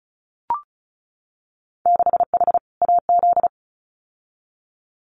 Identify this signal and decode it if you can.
Morse code, heard as a keyed tone, about 34 wpm, 700 Hz. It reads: 6H AZ
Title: Winter Field Day Exchanges Encore 35wpm